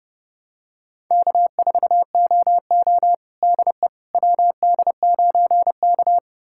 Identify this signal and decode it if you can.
Morse code, heard as a keyed tone, about 30 wpm, 700 Hz. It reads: K4OO DE WD9K
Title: USA Callsign DE Callsign - 4 Characters - 30wpm